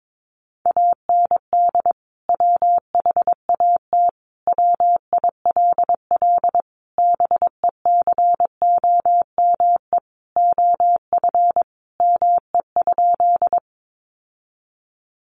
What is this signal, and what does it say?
Morse code, heard as a keyed tone, about 22 wpm, 700 Hz. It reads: AND WHAT WILL BECOME OF ME?
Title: Sentences from Top 600 Words 22wpm